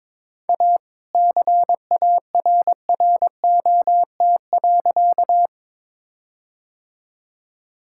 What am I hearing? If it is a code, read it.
Morse code, heard as a keyed tone, about 23 wpm, 700 Hz. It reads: A CARROT.
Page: morse-code-ninja-practice-22wpm